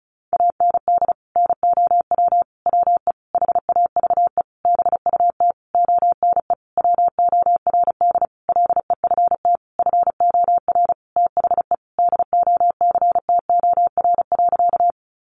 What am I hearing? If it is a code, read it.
Morse code, heard as a keyed tone, about 34 wpm, 700 Hz. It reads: AND NOW WE HAVE BUT ONE WORD LEFT FOR THE DOCTOR.
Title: Sentences from Top 1500 Words 35wpm